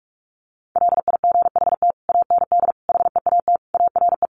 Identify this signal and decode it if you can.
Morse code, heard as a keyed tone, about 44 wpm, 700 Hz. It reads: LIGHT AND HEAT ARE
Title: Sentences from Top 1300 Words 45wpm